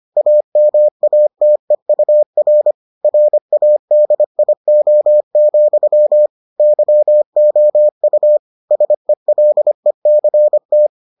Morse code, heard as a keyed tone, about 25 wpm, 600 Hz. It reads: AMATEUR RADIO, YOU SELECT